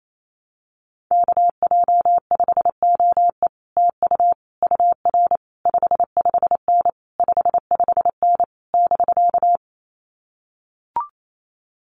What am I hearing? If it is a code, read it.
Morse code, heard as a keyed tone, about 28 wpm, 700 Hz. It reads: KJ5OE TU UR 55N 55N <BK>
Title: Common QSO Phrases Encore 28wpm